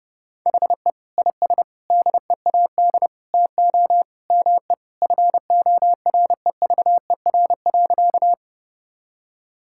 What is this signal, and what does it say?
Morse code, heard as a keyed tone, about 32 wpm, 700 Hz. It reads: HE IS DEAD TO ME FOREVER.
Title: Sentences from Top 1600 Words 30wpm